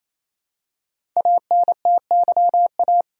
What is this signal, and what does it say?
Morse code, heard as a keyed tone, about 29 wpm, 700 Hz. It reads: ANTYA
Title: Code Groups - 5 Letter:Numbers - 28wpm